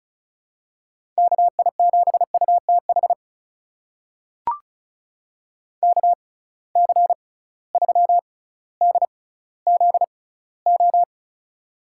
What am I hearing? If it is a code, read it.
Morse code, heard as a keyed tone, about 35 wpm, spaced as for 18 wpm, 700 Hz. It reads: KI7UTH KC3DZO
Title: USA Call Signs - 6 Characters - 6x Character Spacing - 35wpm